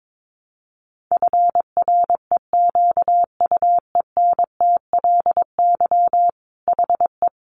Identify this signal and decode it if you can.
Morse code, heard as a keyed tone, about 22 wpm, 700 Hz. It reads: FREQUENTLY HE
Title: Sentences from Top 1300 Words 22wpm